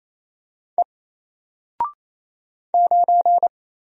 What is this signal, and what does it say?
Morse code, heard as a keyed tone, about 28 wpm, 700 Hz. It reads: E 9
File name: Single-Letter-Number-28wpm